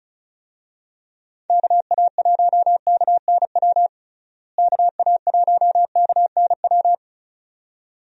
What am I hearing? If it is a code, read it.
Morse code, heard as a keyed tone, about 36 wpm, 700 Hz. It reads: KA1KNW KA1KNW
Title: USA Call Signs - 6 Characters - Repeated 2x - 35wpm